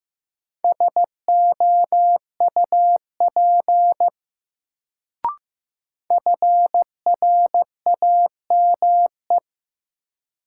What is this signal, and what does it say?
Morse code, heard as a keyed tone, about 15 wpm, 700 Hz. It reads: SOUP FRAME
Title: Sentences from Top 1000 Words - Review Part 6 - New Word Drill - 15wpm